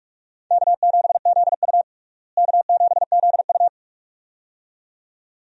Morse code, heard as a keyed tone, about 43 wpm, 700 Hz. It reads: K7ZU K7ZU
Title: USA Call Signs - 4 Characters - Repeated 2x - 45wpm